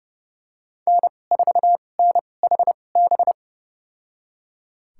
Morse code, heard as a keyed tone, about 31 wpm, spaced as for 22 wpm, 700 Hz. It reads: N4NHB
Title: USA Call Signs - 5 Characters - 2x Character Spacing - 30wpm